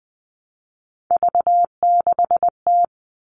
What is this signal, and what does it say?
Morse code, heard as a keyed tone, about 20 wpm, 700 Hz. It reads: V6T
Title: All Cut Numbers - 3-Digit 20wpm